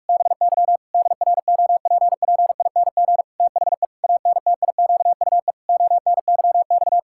Morse code, heard as a keyed tone, about 46 wpm, 700 Hz. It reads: BY DROPPING THE ANTIQUE ONYX
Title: Pangram Challenge v2 - 45wpm